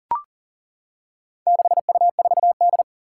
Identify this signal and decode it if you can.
Morse code, heard as a keyed tone, about 40 wpm, 700 Hz. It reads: 6U4D
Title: Four Challenging Letter-Number Combinations 40wpm